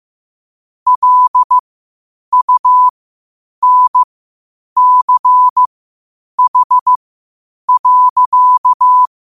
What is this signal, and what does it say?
Morse code, heard as a keyed tone, about 15 wpm, spaced as for 8 wpm, 1000 Hz. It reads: LUNCH.
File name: qotdfiles / 8wpm